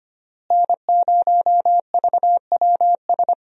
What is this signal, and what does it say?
Morse code, heard as a keyed tone, about 25 wpm, 700 Hz. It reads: N0VWS
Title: USA Call Signs - 5 Characters - Repeated 3x - 25wpm